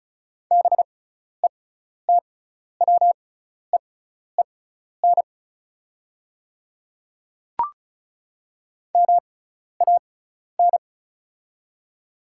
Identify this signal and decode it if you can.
Morse code, heard as a keyed tone, about 34 wpm, spaced as for 12 wpm, 690 Hz. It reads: BETWEEN MAN
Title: 6x Character Spacing - Top 500 Words - 35wpm